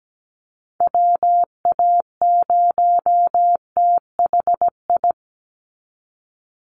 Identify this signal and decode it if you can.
Morse code, heard as a keyed tone, about 17 wpm, 700 Hz. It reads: WA0THI
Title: USA Call Signs 17wpm